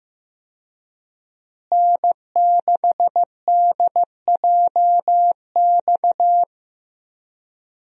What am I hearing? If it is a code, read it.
Morse code, heard as a keyed tone, about 15 wpm, 700 Hz. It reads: N6DJX
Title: USA Call Signs - 5 Characters - Repeated 2x - 15wpm